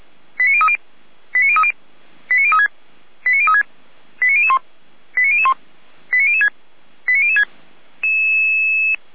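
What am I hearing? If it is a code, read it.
Five-tone selective call (ZVEI), heard as quick runs of five tones, tones of about 70 ms. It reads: 89030 89030 89036 89036 89001 89001 89006 89006
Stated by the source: ZVEI five-tone call 89099 received on an SDR, logged by feuerwehr-orlamuende.org (fire brigade Orlamünde).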